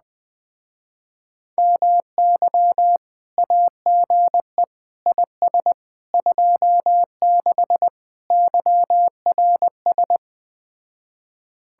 Morse code, heard as a keyed tone, about 20 wpm, 700 Hz. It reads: MY AGE IS 26 YRS